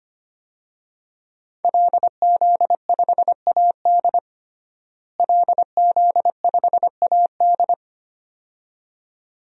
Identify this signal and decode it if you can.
Morse code, heard as a keyed tone, about 25 wpm, 700 Hz. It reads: LZ5AD LZ5AD
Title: DXCC-100 Call Signs - 5 Characters - Repeated 2x - 25wpm